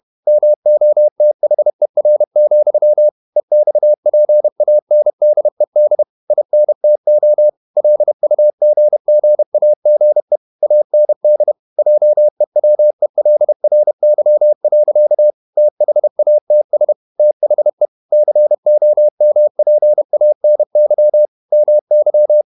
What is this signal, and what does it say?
Morse code, heard as a keyed tone, about 31 wpm, 600 Hz. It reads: MOTHER, EXPANDED INTO LUGGAGE AND JEWELRY. THATS THE COMPANY MY
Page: Never 31wpm | CW med Gnister